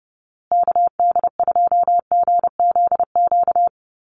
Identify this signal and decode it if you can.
Morse code, heard as a keyed tone, about 30 wpm, 700 Hz. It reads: KD2GZQ
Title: USA Call Signs Repeated 3x 30wpm